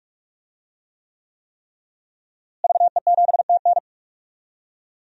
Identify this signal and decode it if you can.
Morse code, heard as a keyed tone, about 41 wpm, 700 Hz. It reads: VE7TN
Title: DXCC-100 Call Signs - 5 Characters - Repeated 3x - 45wpm